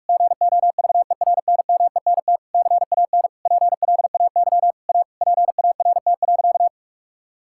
Morse code, heard as a keyed tone, about 46 wpm, 700 Hz. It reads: GOVERNMENT CAN PLAY A PART.
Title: Sentences from Top 1000 Words - Review Part 5 - 45wpm